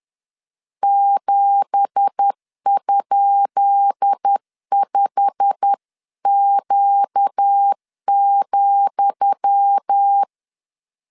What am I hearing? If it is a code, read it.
Morse code, heard as a keyed tone, about 10 wpm, 800 Hz. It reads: 7?5Q,
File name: LEZIONI / 10 SIGNS / 10D COMA 10wpm 3013